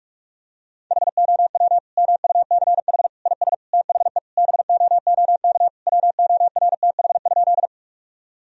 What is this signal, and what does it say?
Morse code, heard as a keyed tone, about 46 wpm, 700 Hz. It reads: HOW MUCH IS THE BOOK WORTH?
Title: Sentences from Top 1000 Words - Review Part 1 - 45wpm